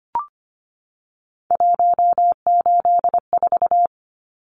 Morse code, heard as a keyed tone, about 25 wpm, 700 Hz. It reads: 184
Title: Numbers - 3-Digit 25wpm